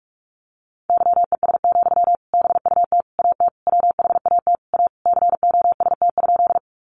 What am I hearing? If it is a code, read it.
Morse code, heard as a keyed tone, about 44 wpm, 700 Hz. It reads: YES, BUT AT WHAT A COST?
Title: Sentences from Top 800 Words 45wpm